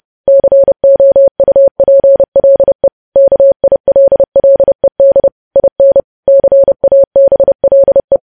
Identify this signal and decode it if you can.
Morse code, heard as a keyed tone, about 30 wpm, 550 Hz. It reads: COUPLE KILLED IN CABLE
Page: news headlines in morse code at 30 wpm